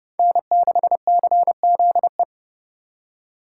Morse code, heard as a keyed tone, about 28 wpm, 700 Hz. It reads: N6CZE